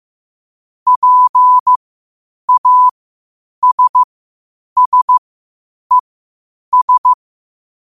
Morse code, heard as a keyed tone, about 15 wpm, spaced as for 8 wpm, 1000 Hz. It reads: PASSES
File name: qotdfiles / 8wpm